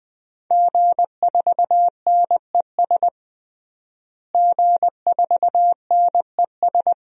Morse code, heard as a keyed tone, about 20 wpm, 700 Hz. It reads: G4NES G4NES
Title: DXCC-100 Call Signs - 5 Characters - Repeated 2x - 20wpm